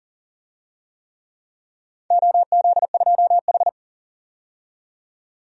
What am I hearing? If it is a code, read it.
Morse code, heard as a keyed tone, about 38 wpm, 700 Hz. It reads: OZ2H
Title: DXCC-100 Call Signs - 4 Characters - 40wpm